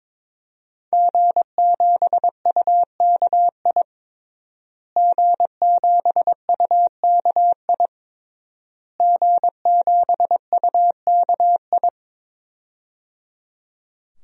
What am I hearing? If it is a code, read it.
Morse code, heard as a keyed tone, about 22 wpm, 700 Hz. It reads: G7UKI G7UKI G7UKI
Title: DXCC-100 Call Signs - 5 Characters - Repeated 3x - 22wpm